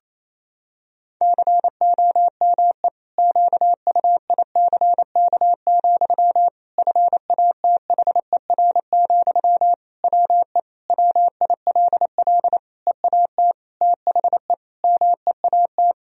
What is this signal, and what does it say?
Morse code, heard as a keyed tone, about 27 wpm, 700 Hz. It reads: COME QUICK, FATHER, WE WILL EAT THE MEAT
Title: Sentences from Top 1000 Words - Review Part 3 - 28wpm